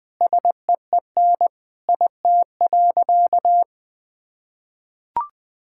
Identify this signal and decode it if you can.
Morse code, heard as a keyed tone, about 20 wpm, 700 Hz. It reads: SEEN IT.
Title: Sentences with 5-Word Phrases - Spoken Beginning - 20wpm